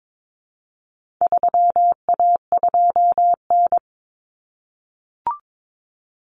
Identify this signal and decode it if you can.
Morse code, heard as a keyed tone, about 22 wpm, 700 Hz. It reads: 3A2N
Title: Common Cut Numbers - 4-Digit 22wpm